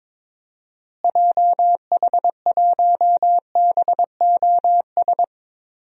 Morse code, heard as a keyed tone, about 22 wpm, 700 Hz. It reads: JH1BOS
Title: DXCC-100 Call Signs - 6 Characters - Repeated 2x - 22wpm